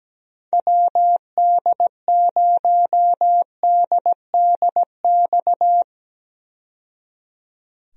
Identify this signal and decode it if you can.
Morse code, heard as a keyed tone, about 17 wpm, 700 Hz. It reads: WD0DDX
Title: USA Call Signs - 6 Characters - 17wpm